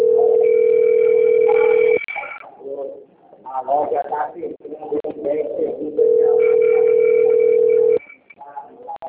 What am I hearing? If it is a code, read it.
Ringback tone (North America), 440 + 480 Hz, with speech.
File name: out- - - - - .9899